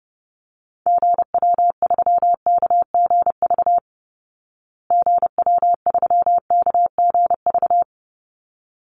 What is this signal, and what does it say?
Morse code, heard as a keyed tone, about 30 wpm, 700 Hz. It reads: GW3KGV GW3KGV
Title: Call Signs Repeated 3x - 30wpm